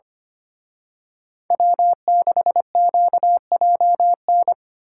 Morse code, heard as a keyed tone, about 29 wpm, 700 Hz. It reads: W6QJN
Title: Speed Racing - USA 1x3 Call Signs 25wpm